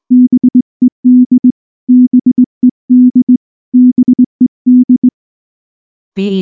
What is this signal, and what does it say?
Morse code, heard as a keyed tone, about 20 wpm, 250 Hz. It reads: BED BED BED
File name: morse / morse-training-20WPM-4